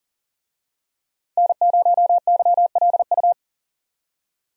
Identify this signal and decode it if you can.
Morse code, heard as a keyed tone, about 40 wpm, 700 Hz. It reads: N0YLU
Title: USA Call Signs - 5 Characters - 40wpm